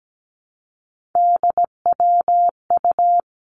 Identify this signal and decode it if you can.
Morse code, heard as a keyed tone, about 17 wpm, 700 Hz. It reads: DWU
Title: Three Challenging Letter-Number Combinations 17wpm